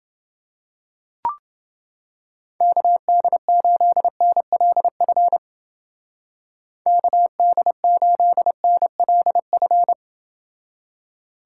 Morse code, heard as a keyed tone, about 31 wpm, 700 Hz. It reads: KD8NLF KD8NLF